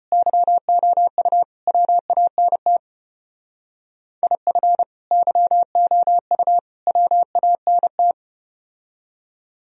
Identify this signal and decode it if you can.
Morse code, heard as a keyed tone, about 32 wpm, 700 Hz. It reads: YOU WANT IF YOU WANT